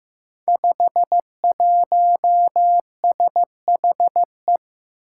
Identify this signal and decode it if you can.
Morse code, heard as a keyed tone, about 15 wpm, 700 Hz. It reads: 51SHE
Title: Five Challenging Letter-Number Combinations 15wpm